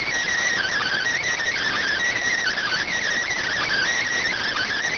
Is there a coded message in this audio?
Yes: Emergency Alert System (SAME) tones (header bursts) from 0.5 to 4.7 s, heard as harsh buzzing bursts.